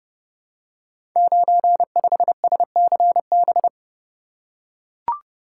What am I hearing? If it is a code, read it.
Morse code, heard as a keyed tone, about 30 wpm, 700 Hz. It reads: 95SCB
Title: Code Groups - 5 Letter:Numbers - 30wpm